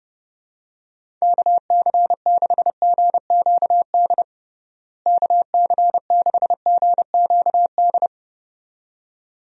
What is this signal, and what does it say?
Morse code, heard as a keyed tone, about 30 wpm, 700 Hz. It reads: KC6GQD KC6GQD
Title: USA Call Signs - 6 Characters - Repeated 2x - 30wpm